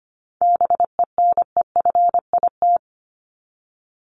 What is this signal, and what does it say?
Morse code, heard as a keyed tone, about 25 wpm, 700 Hz. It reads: BENEFIT
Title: Top 751-1000 Words 25wpm